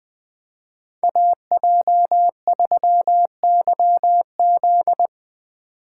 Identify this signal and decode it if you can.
Morse code, heard as a keyed tone, about 22 wpm, 700 Hz. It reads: AJ3YZ
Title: Speed Racing - USA 2x2 Call Signs 20wpm